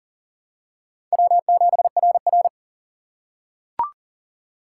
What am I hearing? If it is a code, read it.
Morse code, heard as a keyed tone, about 40 wpm, 700 Hz. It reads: W7RR
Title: USA Call Signs - 4 Characters - 3x Character Spacing - 40wpm